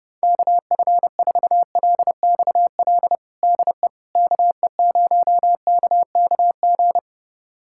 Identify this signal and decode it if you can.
Morse code, heard as a keyed tone, about 30 wpm, 700 Hz. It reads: KF4LXL DE KE0KKG
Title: USA Callsign DE Callsign - 6 Characters - 30wpm